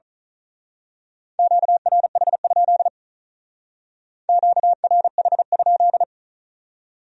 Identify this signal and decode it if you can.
Morse code, heard as a keyed tone, about 36 wpm, 700 Hz. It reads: QRH? QRH?
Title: Speed-Racing Q-Codes - 35wpm